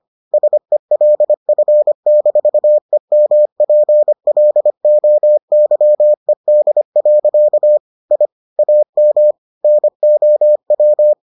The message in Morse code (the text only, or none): SELF-EMPLOYED. I AM NOW